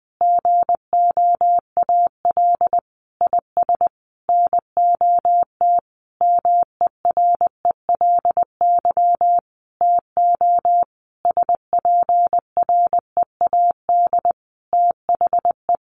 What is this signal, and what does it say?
Morse code, heard as a keyed tone, about 20 wpm, 700 Hz. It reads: GOAL IS NOT MERELY TO SPREAD THE
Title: Sentences from Top 1400 Words 20wpm